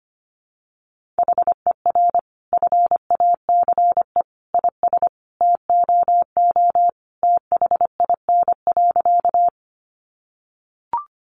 Morse code, heard as a keyed tone, about 25 wpm, 700 Hz. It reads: HER FACE IS TOO THIN.
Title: Sentences from Top 1200 Words 25wpm